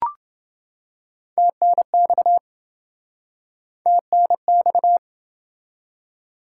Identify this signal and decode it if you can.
Morse code, heard as a keyed tone, about 26 wpm, 700 Hz. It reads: TNX TNX